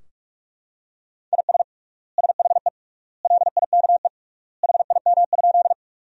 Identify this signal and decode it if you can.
Morse code, heard as a keyed tone, about 46 wpm, 700 Hz. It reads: IS SHE LIKE HIM?
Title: 3x Word-Spacing - Sentences from Top 100 Words - 45wpm